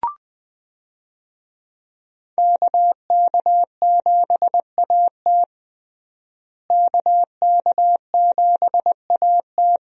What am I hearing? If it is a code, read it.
Morse code, heard as a keyed tone, about 20 wpm, 700 Hz. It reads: KK7AT KK7AT
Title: USA Call Signs - 5 Characters - Repeated 3x - 20wpm